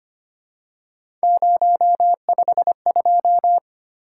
Morse code, heard as a keyed tone, about 25 wpm, 700 Hz. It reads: 052
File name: Three Letter-Number Combinations - 25wpm